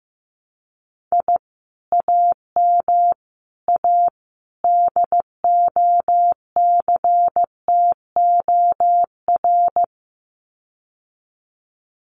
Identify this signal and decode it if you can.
Morse code, heard as a keyed tone, about 15 wpm, 700 Hz. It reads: I AM A DOCTOR
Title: Common QSO Phrases Encore 15wpm